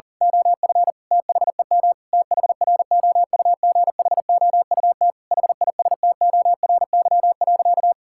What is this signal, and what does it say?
Morse code, heard as a keyed tone, about 40 wpm, 700 Hz. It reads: OF THEM THROUGHOUT HISTORY.
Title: Sentences from Top 1000 Words - Review Part 5 - 40wpm